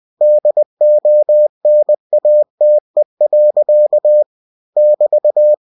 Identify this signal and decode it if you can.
Morse code, heard as a keyed tone, about 20 wpm, 600 Hz. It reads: DONATE. <BT>